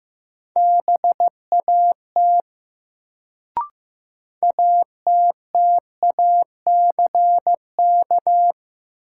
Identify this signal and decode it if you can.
Morse code, heard as a keyed tone, about 15 wpm, 700 Hz. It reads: BAT ATTACK
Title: Sentences from Top 1500 Words - New Word Drill Alt - 15wpm